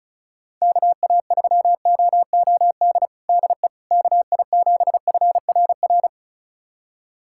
Morse code, heard as a keyed tone, about 36 wpm, 700 Hz. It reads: KA3OOD DE KI7FRR
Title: USA Callsign DE Callsign 35wpm